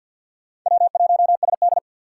Morse code, heard as a keyed tone, about 46 wpm, 700 Hz. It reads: W1SD
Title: USA Call Signs - 4 Characters - 6x Character Spacing - 50wpm